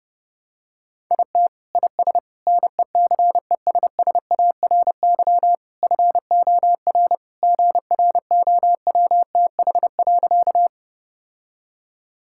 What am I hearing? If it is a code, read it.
Morse code, heard as a keyed tone, about 32 wpm, 700 Hz. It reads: IT IS NECESSARY FOR GROWTH.
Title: Sentences from Top 1000 Words - Review Part 5 - 30wpm